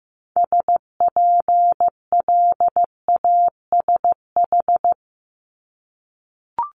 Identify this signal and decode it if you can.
Morse code, heard as a keyed tone, about 15 wpm, 700 Hz. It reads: SPLASH
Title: Multi-letter Phonemes 15wpm